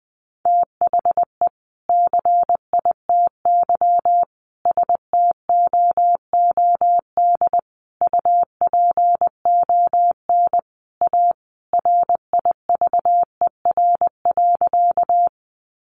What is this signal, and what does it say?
Morse code, heard as a keyed tone, about 20 wpm, 700 Hz. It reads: THE CITY STOOD UPON A RIVER.
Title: Sentences from Top 1100 Words 20wpm